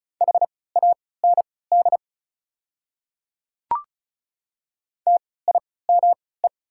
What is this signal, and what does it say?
Morse code, heard as a keyed tone, about 36 wpm, spaced as for 20 wpm, 690 Hz. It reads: HAND TIME
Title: 3x Character Spacing - Top 500 Words - 35wpm